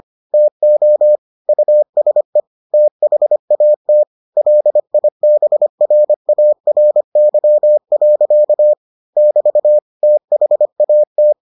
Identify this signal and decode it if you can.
Morse code, heard as a keyed tone, about 25 wpm, 600 Hz. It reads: TO USE THAT LIBRARY. <BT> THAT